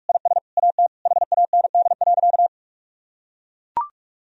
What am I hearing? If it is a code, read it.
Morse code, heard as a keyed tone, about 41 wpm, 700 Hz. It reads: IS AT HAND.
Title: Sentences from Top 1000 Words - Review Part 4 - 45wpm